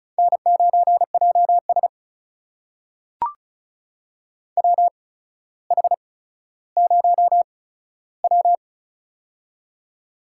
Morse code, heard as a keyed tone, about 36 wpm, spaced as for 15 wpm, 700 Hz. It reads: N9JS WH0W
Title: USA Call Signs - 4 Characters - 8x Character Spacing - 35wpm